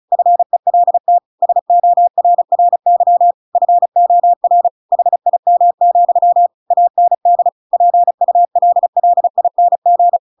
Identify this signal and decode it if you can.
Morse code, heard as a keyed tone, about 34 wpm, 700 Hz. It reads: FELT SORRY FOR HIM, AND PULLING
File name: Pinocchio / Pinocchio ch7 35 35 ik1hge